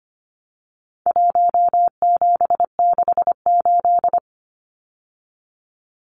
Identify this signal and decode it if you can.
Morse code, heard as a keyed tone, about 25 wpm, 700 Hz. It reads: 1768
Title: Numbers - Contest Style - 4-Digit 25wpm